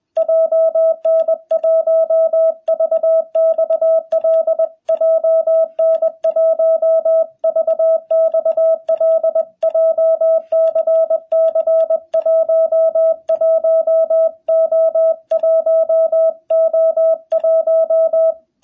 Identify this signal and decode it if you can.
Morse code, heard as a keyed tone, about 21 wpm, 660 Hz. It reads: JN1VXL JN1VXL JCC 11O1O1
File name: jh8cbh / 2021cw